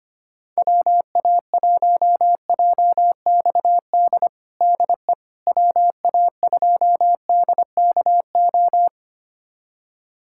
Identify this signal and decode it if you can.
Morse code, heard as a keyed tone, about 25 wpm, 700 Hz. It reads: WA1JXD DE WA2DKO
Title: USA Callsign DE Callsign - 6 Characters - 25wpm